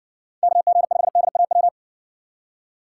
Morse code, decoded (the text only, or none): KG5DAW